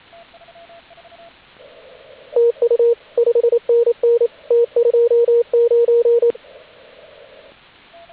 Morse code, heard as a keyed tone, about 27 wpm, 470 Hz, with radio noise. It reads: TU 5NN T29
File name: AUDIO2019 / HG8DX CW